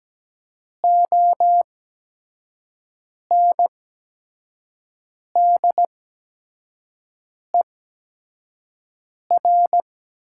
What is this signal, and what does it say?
Morse code, heard as a keyed tone, about 17 wpm, spaced as for 5 wpm, 700 Hz. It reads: ONDER
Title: 8x Character Spacing - Top 500 Words - 17wpm